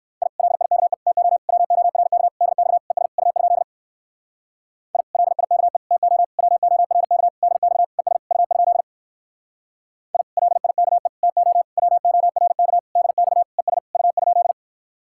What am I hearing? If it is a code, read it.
Morse code, heard as a keyed tone, about 63 wpm, 700 Hz. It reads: I LIKE TO WORK DX ES U? I LIKE TO WORK DX ES U? I LIKE TO WORK DX ES U?